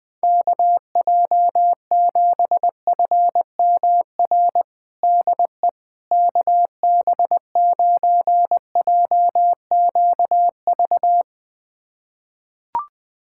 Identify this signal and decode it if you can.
Morse code, heard as a keyed tone, about 20 wpm, 700 Hz. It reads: KJ7FMR DE KB9JQV
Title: USA Callsign DE Callsign 20wpm